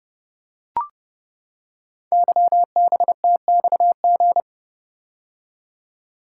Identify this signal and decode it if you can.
Morse code, heard as a keyed tone, about 28 wpm, 700 Hz. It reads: YBTXG
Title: Code Groups - 5 Letters - 30wpm